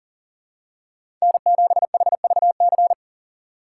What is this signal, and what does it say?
Morse code, heard as a keyed tone, about 40 wpm, 700 Hz. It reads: N7HVC
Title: USA Call Signs - 5 Characters - 3x Character Spacing - 40wpm